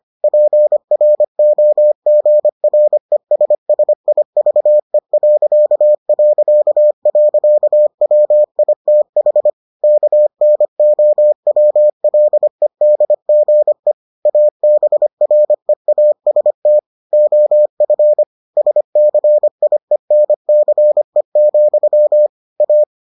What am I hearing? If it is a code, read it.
Morse code, heard as a keyed tone, about 25 wpm, 600 Hz. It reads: PROGRESSIVE...WITH KNOWLEDGE ABREAST OF SCIENCE, A